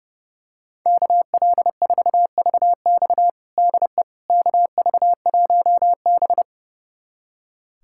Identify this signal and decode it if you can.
Morse code, heard as a keyed tone, about 30 wpm, 700 Hz. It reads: KL4VX DE KV1B
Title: USA Callsign DE Callsign 30wpm